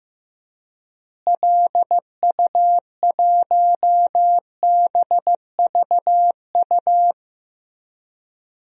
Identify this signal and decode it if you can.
Morse code, heard as a keyed tone, about 15 wpm, 700 Hz. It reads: LU1BVU
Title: DXCC-100 Call Signs - 6 Characters - Repeated 3x - 15wpm